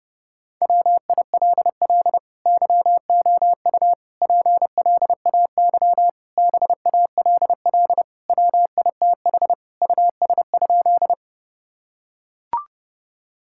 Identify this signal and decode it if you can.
Morse code, heard as a keyed tone, about 30 wpm, 700 Hz. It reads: WILL YOU PLAY BALL WITH US?
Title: Sentences from Top 1000 Words - Review Part 6 - 30wpm